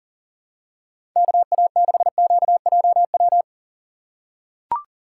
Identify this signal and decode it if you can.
Morse code, heard as a keyed tone, about 38 wpm, 700 Hz. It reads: KA6QJW
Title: USA Call Signs - 2x Character Spacing - 40wpm